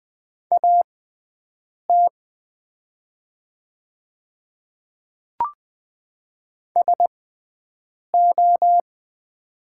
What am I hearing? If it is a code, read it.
Morse code, heard as a keyed tone, about 20 wpm, spaced as for 7 wpm, 690 Hz. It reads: AT SO
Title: 6x Character Spacing - Top 100 Words - 20wpm